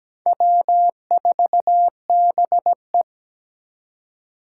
Morse code, heard as a keyed tone, about 20 wpm, 700 Hz. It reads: W4BE
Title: Speed Racing - USA 1x2 Call Signs 17wpm